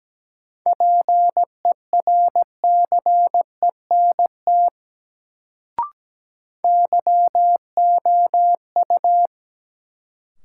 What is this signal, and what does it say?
Morse code, heard as a keyed tone, about 17 wpm, 700 Hz. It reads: PERCENT YOU
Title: Sets of 2 Words - New Word Drill - 17wpm